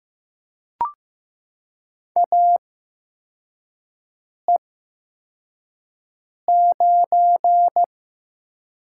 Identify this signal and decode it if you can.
Morse code, heard as a keyed tone, about 15 wpm, spaced as for 4 wpm, 700 Hz. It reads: AE9
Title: USA Call Signs - 5 Characters - 8x Character Spacing - 15wpm